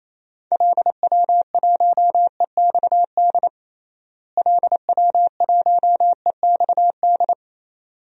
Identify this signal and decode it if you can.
Morse code, heard as a keyed tone, about 28 wpm, 700 Hz. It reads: LW1EXD LW1EXD
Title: DXCC-100 Call Signs - 6 Characters - Repeated 2x - 28wpm